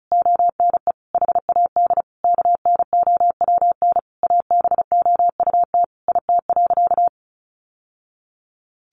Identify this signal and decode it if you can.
Morse code, heard as a keyed tone, about 34 wpm, 700 Hz. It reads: ONE HAD KNOWN ABOUT IT.
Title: Sentences from Top 1200 Words 35wpm